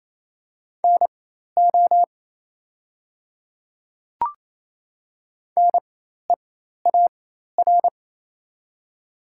Morse code, heard as a keyed tone, about 28 wpm, spaced as for 13 wpm, 700 Hz. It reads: NO NEAR
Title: 4x Character Spacing - Top 300 Words - 28wpm